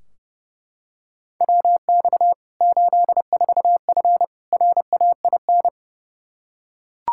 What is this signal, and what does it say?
Morse code, heard as a keyed tone, about 30 wpm, 700 Hz. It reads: WX 84F RAIN
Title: K1USN SST QSO Practice 30wpm